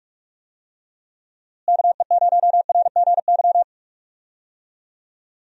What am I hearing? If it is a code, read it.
Morse code, heard as a keyed tone, about 46 wpm, 700 Hz. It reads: KE0RGY
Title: USA Call Signs - 6 Characters - Repeated 2x - 45wpm